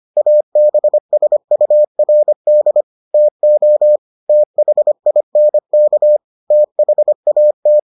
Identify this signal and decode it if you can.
Morse code, heard as a keyed tone, about 25 wpm, 600 Hz. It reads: ABSURD TO THINK THAT